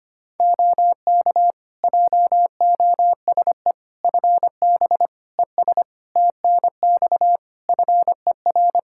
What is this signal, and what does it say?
Morse code, heard as a keyed tone, about 25 wpm, 700 Hz. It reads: OK JOSE FB ES TNX FER
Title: K1USN SST QSO Practice 25wpm